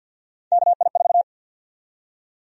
Morse code, heard as a keyed tone, about 52 wpm, 700 Hz. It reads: KI4
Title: USA Call Sign Prefixes - 50wpm